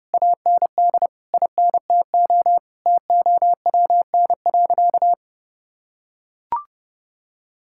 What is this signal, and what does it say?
Morse code, heard as a keyed tone, about 30 wpm, 700 Hz. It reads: AND INTO TOWN.
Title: Sentences from Top 1600 Words 30wpm